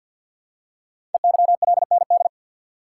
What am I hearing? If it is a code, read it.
Morse code, heard as a keyed tone, about 52 wpm, 700 Hz. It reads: EYLND
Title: Code Groups - 5 Letters - 50wpm